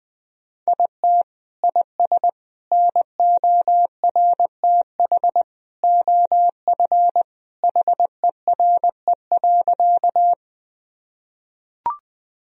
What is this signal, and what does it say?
Morse code, heard as a keyed tone, about 20 wpm, 700 Hz. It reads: IT IS NORTH OF HERE.
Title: Sentences from Top 1000 Words - Review Part 1 - 20wpm